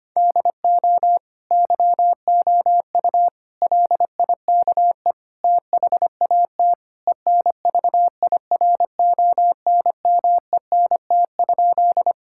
Morse code, heard as a keyed tone, about 25 wpm, 700 Hz. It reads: DO YOU LIKE THAT ENVIRONMENT?
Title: Sentences from Top 1000 Words - Review Part 5 - 25wpm